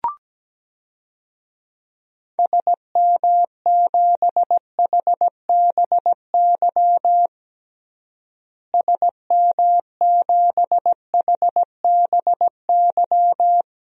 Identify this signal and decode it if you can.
Morse code, heard as a keyed tone, about 17 wpm, 700 Hz. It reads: SM7HBY SM7HBY
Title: DXCC-100 Call Signs - 6 Characters - Repeated 3x - 17wpm